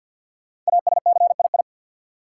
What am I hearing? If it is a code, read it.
Morse code, heard as a keyed tone, about 52 wpm, 700 Hz. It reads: ASCII
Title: Sets of 5 Words - New Word Drill - 50wpm